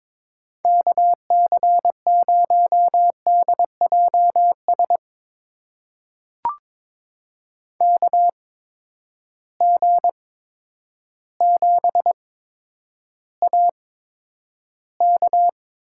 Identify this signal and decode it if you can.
Morse code, heard as a keyed tone, about 22 wpm, spaced as for 10 wpm, 700 Hz. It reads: KC0DJS KG7AK
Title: USA Call Signs - 6 Characters - 8x Character Spacing - 22wpm